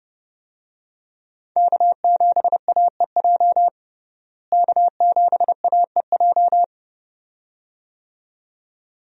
Morse code, heard as a keyed tone, about 29 wpm, 700 Hz. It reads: K7AEJ K7AEJ
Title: USA Call Signs - 5 Characters - Repeated 2x - 30wpm